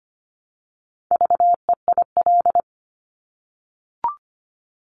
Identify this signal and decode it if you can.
Morse code, heard as a keyed tone, about 25 wpm, 700 Hz. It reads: VEIL
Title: Multi-letter Phonemes Repeated 3x 25wpm